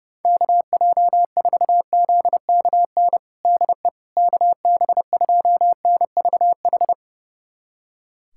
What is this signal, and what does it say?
Morse code, heard as a keyed tone, about 30 wpm, 690 Hz. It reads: KJ4ZKN DE KB2NVH
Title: USA Callsign DE Callsign - 6 Characters - 30wpm